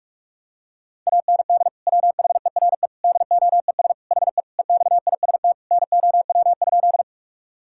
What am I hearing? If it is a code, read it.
Morse code, heard as a keyed tone, about 46 wpm, 700 Hz. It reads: AND WHERE DOES HE EXIST NOW?
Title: Sentences from Top 1000 Words - Review Part 5 - 45wpm